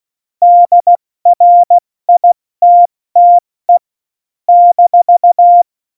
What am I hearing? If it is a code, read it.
Morse code, heard as a keyed tone, about 16 wpm, 700 Hz. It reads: DRITTE -
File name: Morsekurs / Witze / Witz 01. Dezember 2020, 70 F80